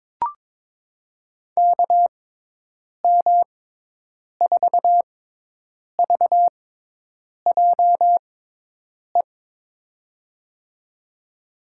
Morse code, heard as a keyed tone, about 22 wpm, spaced as for 8 wpm, 700 Hz. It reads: KM4VJE
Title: USA Call Signs - 6 Characters - 6x Character Spacing - 22wpm